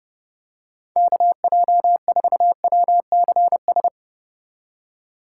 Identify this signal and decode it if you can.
Morse code, heard as a keyed tone, about 31 wpm, 700 Hz. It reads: KJ4WCS